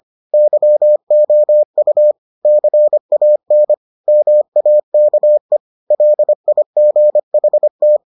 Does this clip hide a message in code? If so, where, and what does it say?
Yes: Morse code from 0.3 to 8.0 s, heard as a keyed tone, about 25 wpm, 600 Hz. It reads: YOU CAN MAKE LIGHT